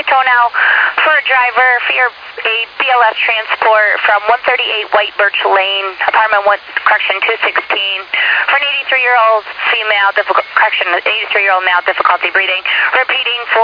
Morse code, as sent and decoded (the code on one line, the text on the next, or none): none